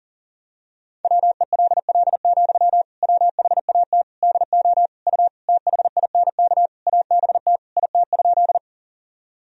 Morse code, heard as a keyed tone, about 40 wpm, 700 Hz. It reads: WELL, WHAT DO U THINK ABT IT?
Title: Sentences from Top 200 Words and Abbreviations 40wpm